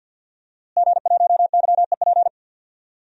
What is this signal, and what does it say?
Morse code, heard as a keyed tone, about 51 wpm, 700 Hz. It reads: G1YEP
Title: DXCC-100 Call Signs - 5 Characters - Repeated 2x - 50wpm